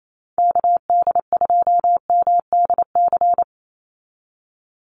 Morse code, heard as a keyed tone, about 28 wpm, 700 Hz. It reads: KD2MDC
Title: USA Call Signs 28wpm